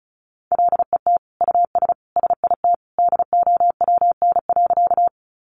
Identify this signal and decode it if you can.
Morse code, heard as a keyed tone, about 34 wpm, 700 Hz. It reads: LET US SIT DOWN.
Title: Sentences from Top 500 Words 35wpm